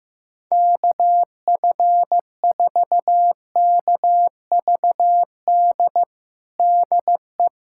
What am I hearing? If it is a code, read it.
Morse code, heard as a keyed tone, about 15 wpm, 700 Hz. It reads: KF4KVD DE
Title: USA Callsign DE Callsign - 6 Characters - 15wpm